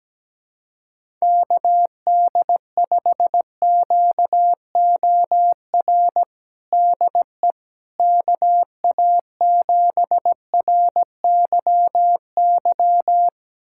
Morse code, heard as a keyed tone, about 17 wpm, 700 Hz. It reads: KD5QOR DE KA7RYY
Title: USA Callsign DE Callsign - 6 Characters - 17wpm